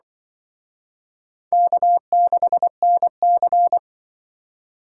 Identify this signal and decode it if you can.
Morse code, heard as a keyed tone, about 25 wpm, 700 Hz. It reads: K6NC
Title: Speed Racing - USA 1x2 Call Signs 20wpm